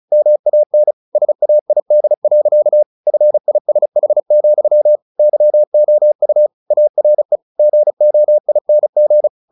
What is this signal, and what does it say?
Morse code, heard as a keyed tone, about 34 wpm, 600 Hz. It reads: MAN SAID. FISH, YOU ARE GOING
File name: the old man and the sea 2 35wpm